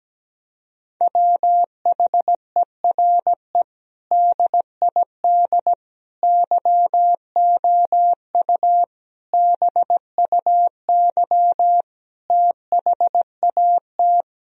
Morse code, heard as a keyed tone, about 17 wpm, 700 Hz. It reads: WHERE DID YOU BUY THAT
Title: Sentences from Top 1000 Words - Review Part 4 - 17wpm